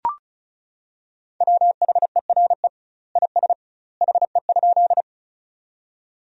Morse code, heard as a keyed tone, about 36 wpm, 700 Hz. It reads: WHERE IS HE?
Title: 2x Word-Spacing - Sentences from Top 200 Words - 35wpm